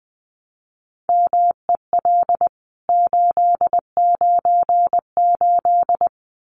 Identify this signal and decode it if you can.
Morse code, heard as a keyed tone, about 20 wpm, 700 Hz. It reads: MEL 898
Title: CWT Exchange 20wpm